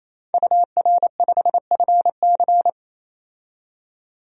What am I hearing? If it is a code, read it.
Morse code, heard as a keyed tone, about 27 wpm, 700 Hz. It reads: UR5FC
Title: DXCC-100 Call Signs - 5 Characters - 28wpm